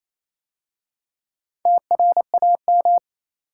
Morse code, heard as a keyed tone, about 27 wpm, 690 Hz. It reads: TRAM
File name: Easy Words - Speed-Racing with Hints - 22wpm